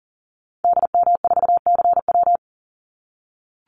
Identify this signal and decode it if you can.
Morse code, heard as a keyed tone, about 40 wpm, 700 Hz. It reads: DM4CW
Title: Call Signs - 40wpm